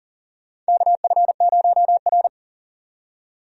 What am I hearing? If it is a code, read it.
Morse code, heard as a keyed tone, about 40 wpm, 700 Hz. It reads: KF0R
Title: USA Call Signs - 4 Characters - 40wpm